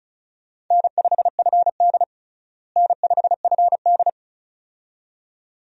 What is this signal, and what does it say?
Morse code, heard as a keyed tone, about 36 wpm, 700 Hz. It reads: N5FD N5FD
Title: USA Call Signs - 4 Characters - Repeated 2x - 35wpm